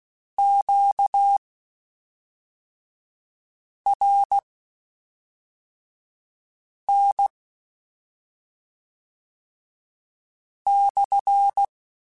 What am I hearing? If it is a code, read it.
Morse code, heard as a keyed tone, about 16 wpm, spaced as for 3 wpm, 800 Hz. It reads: QRN /
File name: Q Calls / Q Calls-mp3s / Q Calls SPSL2 @5